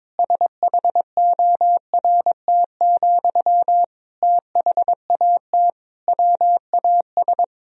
Morse code, heard as a keyed tone, about 22 wpm, 700 Hz. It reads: SHORT, THAT WAS
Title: Sentences from Top 1000 Words - Review Part 2 - 22wpm